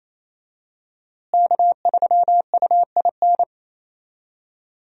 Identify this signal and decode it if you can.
Morse code, heard as a keyed tone, about 28 wpm, 700 Hz. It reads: K3UIN
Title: USA Call Signs - 5 Characters - Repeated 3x - 28wpm